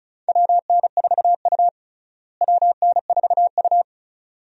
Morse code, heard as a keyed tone, about 34 wpm, 700 Hz. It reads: WN4U WN4U
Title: USA Call Signs - 4 Characters - Repeated 2x - 35wpm